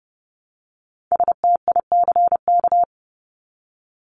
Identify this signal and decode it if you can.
Morse code, heard as a keyed tone, about 30 wpm, 700 Hz. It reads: STICK
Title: Top 1001-1250 Words 30wpm